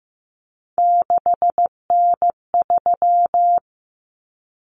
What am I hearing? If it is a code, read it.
Morse code, heard as a keyed tone, about 15 wpm, 700 Hz. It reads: 6N3
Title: Common Cut Numbers - 3-Digit 15wpm